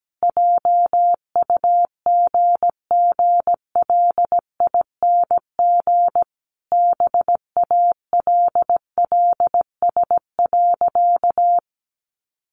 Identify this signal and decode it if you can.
Morse code, heard as a keyed tone, about 17 wpm, 700 Hz. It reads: JUGGLING BALLS.
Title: Easy Phrases 17wpm